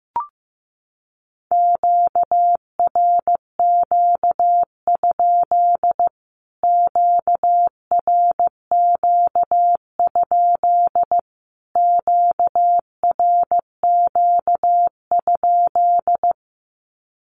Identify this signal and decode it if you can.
Morse code, heard as a keyed tone, about 15 wpm, 700 Hz. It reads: QRQ? QRQ? QRQ?
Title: Q-Codes Repeated 3x 15wpm